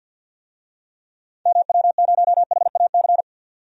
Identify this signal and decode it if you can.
Morse code, heard as a keyed tone, about 50 wpm, 700 Hz. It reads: MW0HAC
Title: DXCC-100 Call Signs - 6 Characters - 50wpm